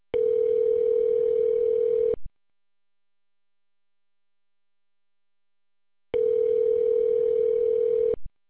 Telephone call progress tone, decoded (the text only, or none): ringback tone